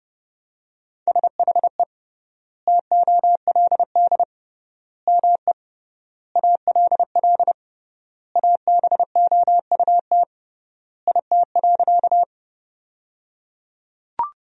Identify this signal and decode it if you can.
Morse code, heard as a keyed tone, about 30 wpm, 700 Hz. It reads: SHE TOLD ME ALL ABOUT IT.
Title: 3x Word-Spacing - Sentences from Top 400 Words - 30wpm